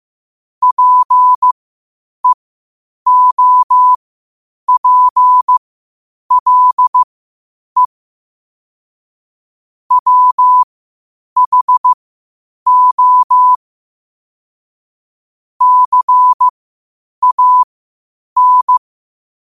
Morse code, heard as a keyed tone, about 15 wpm, spaced as for 8 wpm, 1000 Hz. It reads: PEOPLE WHO CAN
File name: 8wpm